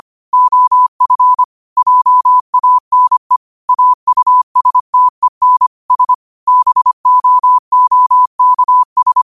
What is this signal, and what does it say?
Morse code, heard as a keyed tone, about 25 wpm, 1000 Hz. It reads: OF JANE AUSTEN S BOOKS
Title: 25 WPM morse code quotes for Thu, 29 Feb 2024